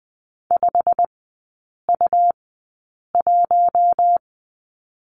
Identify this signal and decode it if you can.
Morse code, heard as a keyed tone, about 20 wpm, spaced as for 8 wpm, 700 Hz. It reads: 5U1
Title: 100 Lesson 17e - Letter u - Three Characters 20wpm